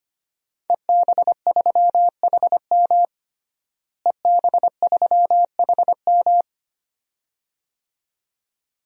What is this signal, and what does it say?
Morse code, heard as a keyed tone, about 25 wpm, 700 Hz. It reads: EB3HM EB3HM
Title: DXCC-100 Call Signs - 5 Characters - Repeated 2x - 25wpm